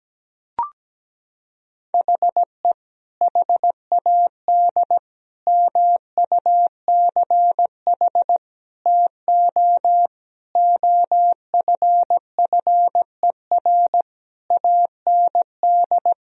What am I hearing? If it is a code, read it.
Morse code, heard as a keyed tone, about 17 wpm, 700 Hz. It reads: HE HAD MUCH TO OFFER AND
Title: Sentences from Top 1000 Words - Review Part 4 - 17wpm